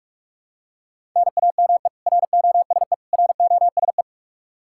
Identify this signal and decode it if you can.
Morse code, heard as a keyed tone, about 46 wpm, 700 Hz. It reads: NAME ROSE ROSE
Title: Practice QSO - Exchange 1 - Beginning - 45wpm